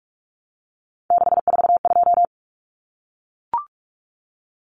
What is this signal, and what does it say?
Morse code, heard as a keyed tone, about 44 wpm, 700 Hz. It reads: 642
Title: Numbers - 3-Digit 45wpm